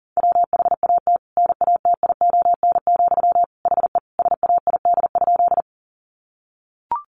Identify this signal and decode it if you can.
Morse code, heard as a keyed tone, about 37 wpm, 700 Hz. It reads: WHAT NATION, HE SAID?
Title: Sentences from Top 900 Words 40wpm